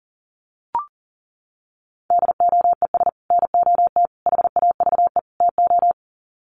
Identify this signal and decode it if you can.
Morse code, heard as a keyed tone, about 40 wpm, 700 Hz. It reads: DOES NOT HAVE TO
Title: Sets of 4 Words 40wpm